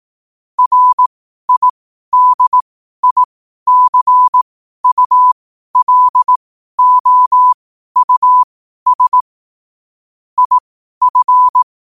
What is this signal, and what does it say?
Morse code, heard as a keyed tone, about 18 wpm, spaced as for 12 wpm, 1000 Hz. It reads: RIDICULOUS IF
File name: qotdfiles / 12wpm